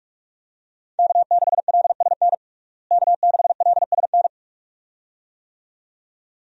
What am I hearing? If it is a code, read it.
Morse code, heard as a keyed tone, about 41 wpm, 700 Hz. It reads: K6LSN K6LSN
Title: USA Call Signs - 5 Characters - Repeated 2x - 45wpm